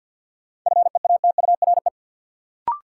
Morse code, heard as a keyed tone, about 48 wpm, 700 Hz. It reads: FEATURE
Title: Sentences from Top 800 Words - New Word Drill - 50wpm